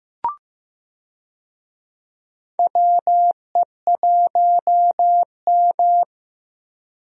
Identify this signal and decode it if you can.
Morse code, heard as a keyed tone, about 15 wpm, 700 Hz. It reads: WE1M
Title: USA Call Signs - 4 Characters - Repeated 3x - 15wpm